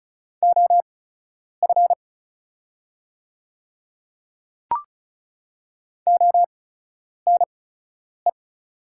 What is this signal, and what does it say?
Morse code, heard as a keyed tone, about 36 wpm, spaced as for 10 wpm, 700 Hz. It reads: OF ONE
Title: 8x Character Spacing - Top 200 Words - 35wpm